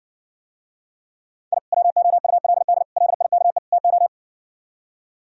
Morse code, heard as a keyed tone, about 68 wpm, 700 Hz. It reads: I WOULD LIKE TO